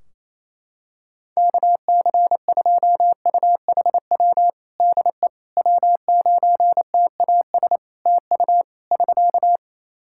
Morse code, heard as a keyed tone, about 27 wpm, 700 Hz. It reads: KC2UHW DE W9TAS TU <SK>
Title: Practice QSO - Exchange 4 - Ending - 28wpm